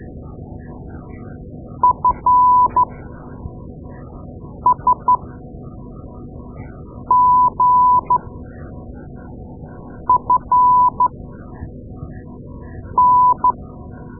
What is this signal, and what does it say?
Morse code, heard as a keyed tone, about 13 wpm, spaced as for 4 wpm, 1000 Hz. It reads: FSGFN
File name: Morse Lessons / Morse Lessons / Tape-2A